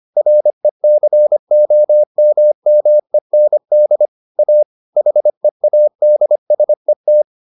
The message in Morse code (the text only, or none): RECOMMEND A HEADSET